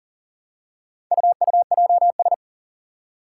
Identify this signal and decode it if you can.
Morse code, heard as a keyed tone, about 40 wpm, 700 Hz. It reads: UUJS
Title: Four Challenging Letter-Number Combinations 40wpm